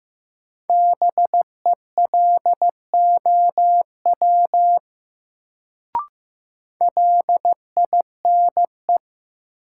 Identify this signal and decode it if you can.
Morse code, heard as a keyed tone, about 15 wpm, 700 Hz. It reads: BELOW LINE
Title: Sentences from Top 1200 Words - New Word Drill Alt - 15wpm